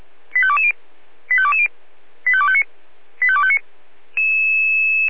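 Five-tone selective call (ZVEI), heard as quick runs of five tones, tones of about 70 ms. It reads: 85220 85220 85259 85259
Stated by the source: ZVEI five-tone call received by SDR.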